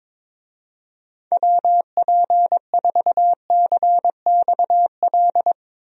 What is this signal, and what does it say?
Morse code, heard as a keyed tone, about 22 wpm, 700 Hz. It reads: WP4CXL
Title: USA Call Signs - 6 Characters - 22wpm